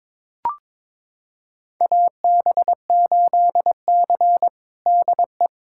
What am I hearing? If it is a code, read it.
Morse code, heard as a keyed tone, about 22 wpm, 700 Hz. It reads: AB8C DE
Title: USA Callsign DE Callsign - 4 Characters - 22wpm